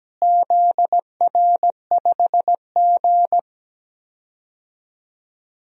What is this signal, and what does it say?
Morse code, heard as a keyed tone, about 17 wpm, 700 Hz. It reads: ZR5G
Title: DXCC-100 Call Signs - 4 Characters - Repeated 3x - 17wpm